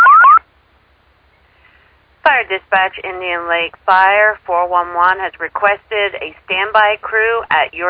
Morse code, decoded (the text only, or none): none